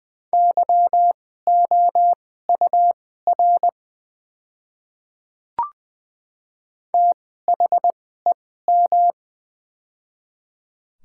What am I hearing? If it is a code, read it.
Morse code, heard as a keyed tone, about 20 wpm, spaced as for 15 wpm, 700 Hz. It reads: YOUR THEM
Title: 2x Character Spacing - Top 100 Words - 20wpm